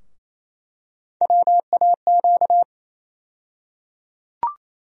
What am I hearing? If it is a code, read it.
Morse code, heard as a keyed tone, about 28 wpm, 700 Hz. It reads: WAQ
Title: USA Call Sign Postfixes - 3 Letters 28wpm